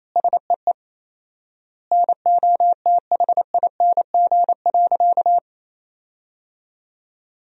Morse code, heard as a keyed tone, about 28 wpm, 700 Hz. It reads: SEE NOTHING.
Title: 4x Word-Spacing - Sentences from Top 500 Words - 28wpm